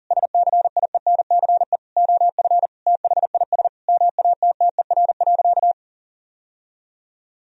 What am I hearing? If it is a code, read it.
Morse code, heard as a keyed tone, about 40 wpm, 700 Hz. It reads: SCIENCE OF THIS MATTER.
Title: Sentences from Top 1000 Words - Review Part 2 - 40wpm